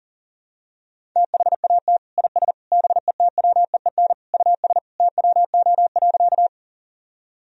Morse code, heard as a keyed tone, about 40 wpm, 700 Hz. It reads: THAT IS BETWEEN US TWO.
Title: Sentences from Top 1000 Words - Review Part 1 - 40wpm